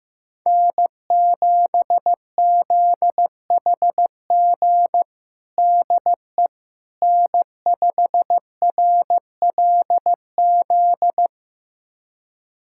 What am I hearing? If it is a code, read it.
Morse code, heard as a keyed tone, about 15 wpm, 700 Hz. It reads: N7ZHG DE N5RLZ
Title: USA Callsign DE Callsign - 5 Characters - 15wpm